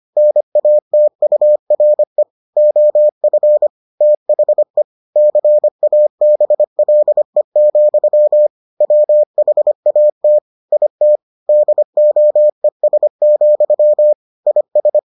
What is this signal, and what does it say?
Morse code, heard as a keyed tone, about 25 wpm, 600 Hz. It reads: NATURE OF THE CABLE, WHAT IT DOES, IS